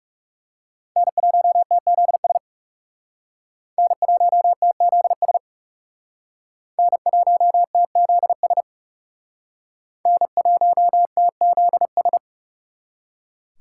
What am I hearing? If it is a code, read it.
Morse code, heard as a keyed tone, about 39 wpm, 700 Hz. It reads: N1TZS N1TZS N1TZS N1TZS